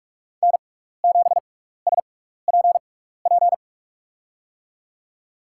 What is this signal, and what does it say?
Morse code, heard as a keyed tone, about 44 wpm, spaced as for 15 wpm, 700 Hz. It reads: N7SPP
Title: USA Call Signs - 5 Characters - 6x Character Spacing - 45wpm